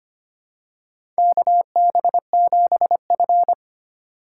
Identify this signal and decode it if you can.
Morse code, heard as a keyed tone, about 25 wpm, 700 Hz. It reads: KB7F
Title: USA Call Signs - 4 Characters - 4x Character Spacing - 25wpm